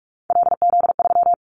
Morse code, heard as a keyed tone, about 44 wpm, 700 Hz. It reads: LZ3
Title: Callsign Prefixes for Top 25 DXCC - 45wpm